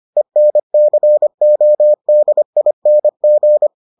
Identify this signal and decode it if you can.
Morse code, heard as a keyed tone, about 25 wpm, 600 Hz. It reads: ENCODING